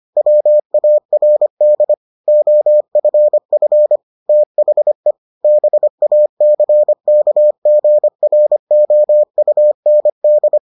Morse code, heard as a keyed tone, about 25 wpm, 600 Hz. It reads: WARD OFF THE BACKGROUND